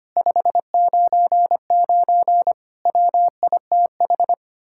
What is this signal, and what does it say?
Morse code, heard as a keyed tone, about 31 wpm, 700 Hz. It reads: 599 WITH
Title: Speed-Racing Common QSO Phrases - 25wpm